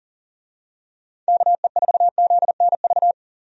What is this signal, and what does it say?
Morse code, heard as a keyed tone, about 40 wpm, 700 Hz. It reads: KE4ZNV
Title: USA Call Signs - 6 Characters - Repeated 3x - 40wpm